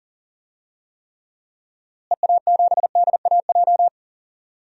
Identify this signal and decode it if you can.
Morse code, heard as a keyed tone, about 40 wpm, 700 Hz. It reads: EA7DAJ
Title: DXCC-100 Call Signs - 6 Characters - Repeated 2x - 40wpm